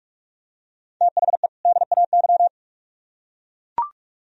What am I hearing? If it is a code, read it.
Morse code, heard as a keyed tone, about 44 wpm, 700 Hz. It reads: THE DAY